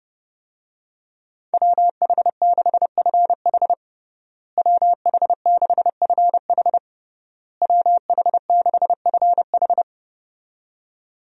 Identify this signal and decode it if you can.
Morse code, heard as a keyed tone, about 30 wpm, 700 Hz. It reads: WH6FH WH6FH WH6FH
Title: USA Call Signs - 5 Characters - Repeated 3x - 30wpm